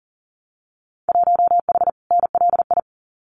Morse code, heard as a keyed tone, about 41 wpm, 700 Hz. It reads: 1H NLI
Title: Winter Field Day Exchanges 40wpm